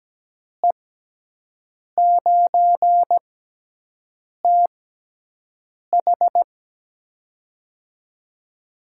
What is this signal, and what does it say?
Morse code, heard as a keyed tone, about 17 wpm, spaced as for 6 wpm, 700 Hz. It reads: E9TH
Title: USA Call Signs - 6x Character Spacing - 17wpm